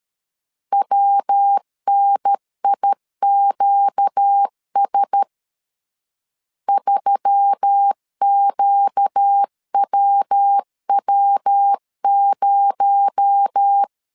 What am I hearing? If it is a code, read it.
Morse code, heard as a keyed tone, about 13 wpm, 800 Hz. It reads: WNIQS 3QWW0
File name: CW / DET / 1 6 1500C S B 12wpm 3431